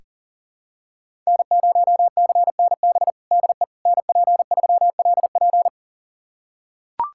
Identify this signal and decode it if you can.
Morse code, heard as a keyed tone, about 37 wpm, 700 Hz. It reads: N0CNB DE NP3LP
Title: USA Callsign DE Callsign - 5 Characters - 40wpm